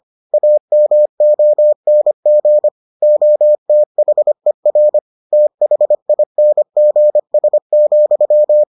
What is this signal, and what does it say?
Morse code, heard as a keyed tone, about 25 wpm, 600 Hz. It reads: AMONG OTHER THINGS,